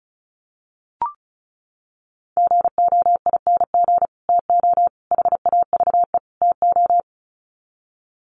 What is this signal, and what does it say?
Morse code, heard as a keyed tone, about 34 wpm, 700 Hz. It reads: GOING TO HAVE TO
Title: Sets of 4 Words 35wpm